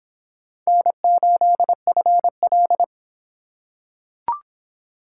Morse code, heard as a keyed tone, about 24 wpm, 700 Hz. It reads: N8FL